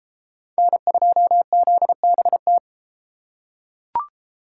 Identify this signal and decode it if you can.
Morse code, heard as a keyed tone, about 32 wpm, 700 Hz. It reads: N2ZBT